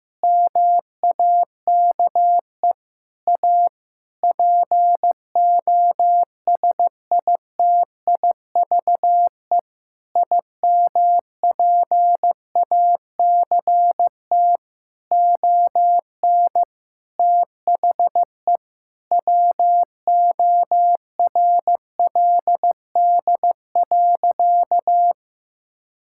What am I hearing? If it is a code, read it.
Morse code, heard as a keyed tone, about 15 wpm, 700 Hz. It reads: MAKE A POSITIVE IMPACT ON THE WORLD.
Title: Sentences from Top 1000 Words - Review Part 6 - 15wpm